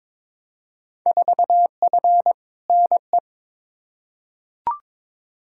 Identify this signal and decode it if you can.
Morse code, heard as a keyed tone, about 22 wpm, 700 Hz. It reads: 4F NE
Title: Field Day Exchanges Encore - 22wpm